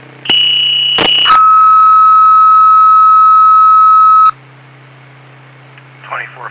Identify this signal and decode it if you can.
Two-tone page, heard as a one-second tone, then a three-second tone, about 2950 then 1290 Hz, with speech. It is heard from 0.3 to 4.3 s.